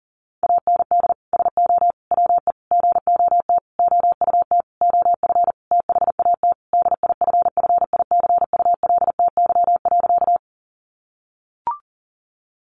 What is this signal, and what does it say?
Morse code, heard as a keyed tone, about 40 wpm, 700 Hz. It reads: AND SO WE GOT OUT OF THAT DIFFICULTY.
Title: Sentences from Top 1500 Words 40wpm